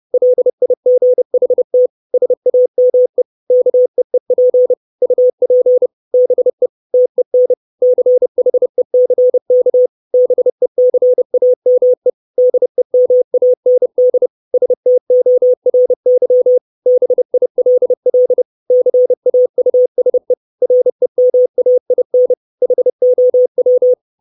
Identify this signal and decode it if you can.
Morse code, heard as a keyed tone, about 30 wpm, 500 Hz. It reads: LIGHT SAME KEEP UP BE TEN CHECK BECAME DEMAND STORY BILL CAUSE REMAIN HOW
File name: 632 Words 30WPM